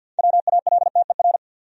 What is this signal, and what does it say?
Morse code, heard as a keyed tone, about 51 wpm, 700 Hz. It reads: WALTER
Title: USA Common Names - 50wpm